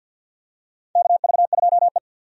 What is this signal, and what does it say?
Morse code, heard as a keyed tone, about 52 wpm, 700 Hz. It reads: KV2E
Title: USA Call Signs - 4 Characters - 50wpm